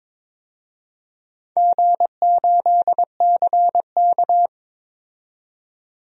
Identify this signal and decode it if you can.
Morse code, heard as a keyed tone, about 22 wpm, 700 Hz. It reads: G8CK
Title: DXCC-100 Call Signs - 4 Characters - 22wpm